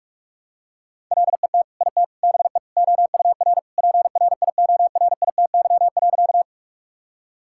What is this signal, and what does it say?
Morse code, heard as a keyed tone, about 46 wpm, 700 Hz. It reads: LET IT BE OUR PRIORITY.
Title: Sentences from Top 1600 Words 45wpm